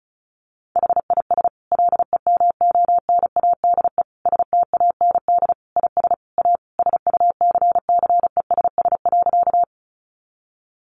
Morse code, heard as a keyed tone, about 34 wpm, 700 Hz. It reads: HIS LEMONADE STAND IS A SUCCESS.
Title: Additional Easy Phrases 35wpm